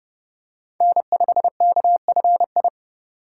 Morse code, heard as a keyed tone, about 29 wpm, 700 Hz. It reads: N5KFI